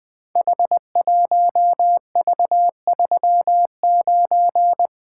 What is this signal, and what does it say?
Morse code, heard as a keyed tone, about 20 wpm, 700 Hz. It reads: H1V39
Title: Code Groups - 5 Letter:Numbers - 20wpm